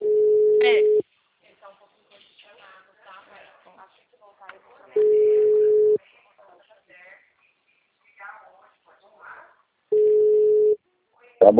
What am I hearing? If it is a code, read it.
Ringback tone (Europe, 425 Hz), with speech.